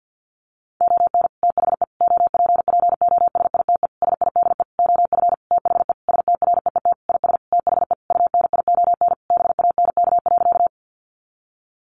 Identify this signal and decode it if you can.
Morse code, heard as a keyed tone, about 50 wpm, 700 Hz. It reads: ON THE OPPOSITE SIDE OF THE STREET IS THE UNION BANK.
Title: Sentences from Top 1400 Words 50wpm